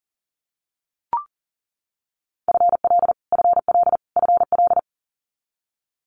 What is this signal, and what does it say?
Morse code, heard as a keyed tone, about 40 wpm, 700 Hz. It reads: FL FL FL
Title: US States Repeated 3x 40wpm